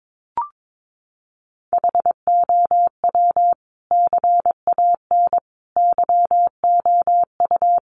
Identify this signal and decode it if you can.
Morse code, heard as a keyed tone, about 22 wpm, 700 Hz. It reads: HOW CAN YOU
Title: Sentences from Top 100 Words 22wpm